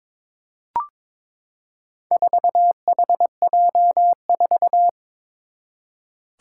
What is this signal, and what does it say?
Morse code, heard as a keyed tone, about 22 wpm, 700 Hz. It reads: 4HJ4
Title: Four Challenging Letter-Number Combinations 22wpm